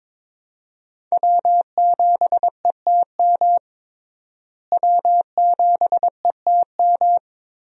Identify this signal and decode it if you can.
Morse code, heard as a keyed tone, about 22 wpm, 700 Hz. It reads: W7ETM W7ETM
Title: USA Call Signs - 5 Characters - Repeated 2x - 22wpm